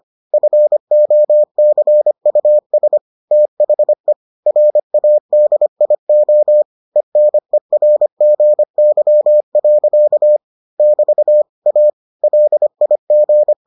Morse code, heard as a keyed tone, about 25 wpm, 600 Hz. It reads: FOCUS THE RADIO ENERGY. <BT> A LIG